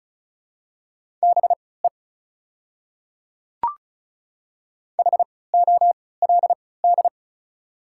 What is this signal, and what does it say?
Morse code, heard as a keyed tone, about 34 wpm, spaced as for 20 wpm, 690 Hz. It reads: BE HOLD
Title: 3x Character Spacing - Top 500 Words - 35wpm